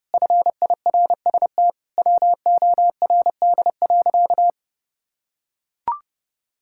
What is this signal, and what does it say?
Morse code, heard as a keyed tone, about 32 wpm, 700 Hz. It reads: FIRST WORD.
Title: Pangram Challenge v2 - 30wpm